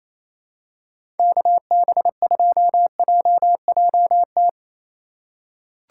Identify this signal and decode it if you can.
Morse code, heard as a keyed tone, about 28 wpm, 690 Hz. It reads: KB2JJT
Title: USA Call Signs - 6 Characters - 2x Character Spacing - 28wpm